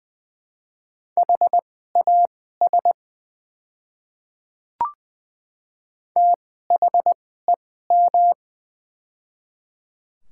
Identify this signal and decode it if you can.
Morse code, heard as a keyed tone, about 20 wpm, spaced as for 14 wpm, 700 Hz. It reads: HAS THEM
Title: 2x Character Spacing - Top 100 Words - 20wpm